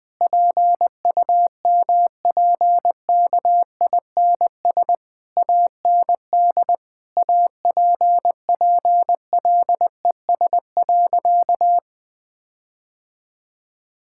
Morse code, heard as a keyed tone, about 19 wpm, 700 Hz. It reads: PUMPKINS AND APPLES.
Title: Difficult Sentences with Hints - 20wpm